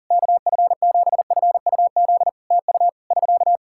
Morse code, heard as a keyed tone, about 40 wpm, 700 Hz. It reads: KF7FUZ TU <SK>
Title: Practice QSO - Exchange 4 - Ending - 40wpm